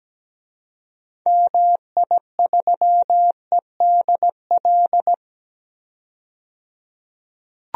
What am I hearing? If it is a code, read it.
Morse code, heard as a keyed tone, about 17 wpm, 700 Hz. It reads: MI3EDL
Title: DXCC-100 Call Signs - 6 Characters - 17wpm